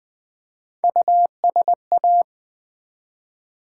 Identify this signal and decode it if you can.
Morse code, heard as a keyed tone, about 20 wpm, 700 Hz. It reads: USA
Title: Build A Word - 1 Letter At A Time - 5-Letter Words - 20wpm